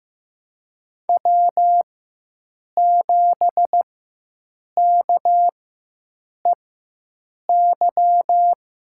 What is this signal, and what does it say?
Morse code, heard as a keyed tone, about 15 wpm, spaced as for 7 wpm, 700 Hz. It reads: W7KEY
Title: USA Call Signs - 4x Character Spacing - 15wpm